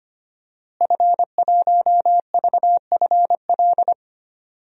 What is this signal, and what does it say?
Morse code, heard as a keyed tone, about 25 wpm, 700 Hz. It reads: F1VFL
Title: DXCC-100 Call Signs - 5 Characters - Repeated 3x - 25wpm